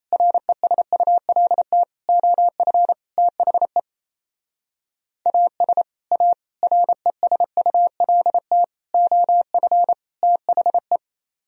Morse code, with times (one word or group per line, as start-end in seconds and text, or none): 0.1-1.8 RESULT
2.1-2.9 OF
3.2-3.8 THE
5.3-5.8 AS
6.1-6.3 A
6.6-8.6 RESULT
8.9-9.9 OF
10.2-11.0 THE